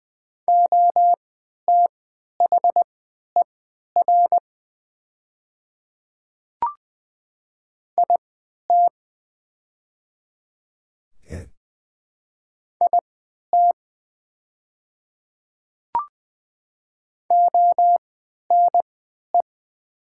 Morse code, heard as a keyed tone, about 20 wpm, spaced as for 11 wpm, 700 Hz. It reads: OTHER IT IT ONE
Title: 3x Character Spacing - Top 100 Words - 20wpm